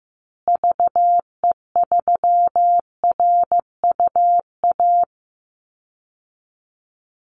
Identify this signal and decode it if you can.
Morse code, heard as a keyed tone, about 15 wpm, 700 Hz. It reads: VE3RUA
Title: Call Signs Repeated 2x - 15wpm